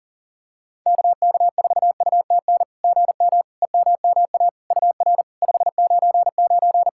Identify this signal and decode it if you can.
Morse code, heard as a keyed tone, about 39 wpm, 700 Hz. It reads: KK4UTN GM EMMA UR 599
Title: SOTA QSO Practice 40wpm